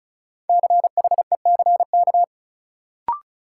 Morse code, heard as a keyed tone, about 36 wpm, 700 Hz. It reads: CHECK
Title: Sentences from Top 1000 Words - Review Part 3 - New Word Drill - 35wpm